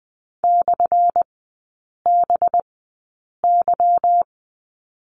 Morse code, heard as a keyed tone, about 20 wpm, spaced as for 8 wpm, 700 Hz. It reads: /BY
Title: 184 Lesson 30e - Letter b - Three Characters 20wpm